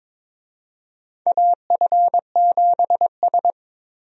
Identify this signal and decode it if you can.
Morse code, heard as a keyed tone, about 21 wpm, 700 Hz. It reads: AF7S
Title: USA Call Signs - 4 Characters - 22wpm